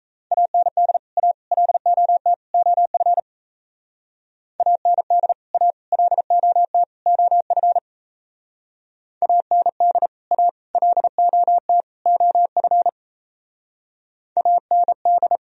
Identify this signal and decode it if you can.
Morse code, heard as a keyed tone, about 36 wpm, 700 Hz. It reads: AND A LOT OF AND A LOT OF AND A LOT OF AND